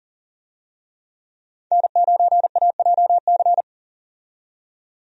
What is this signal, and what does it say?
Morse code, heard as a keyed tone, about 40 wpm, 700 Hz. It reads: N9AJC
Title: USA Call Signs - 5 Characters - 40wpm